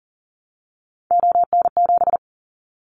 Morse code, heard as a keyed tone, about 40 wpm, 700 Hz. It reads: ON7
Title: Callsign Prefixes for Top 25 DXCC - Repeated 3x - 40wpm